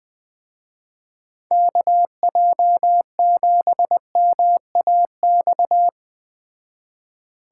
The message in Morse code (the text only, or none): KJ7MAX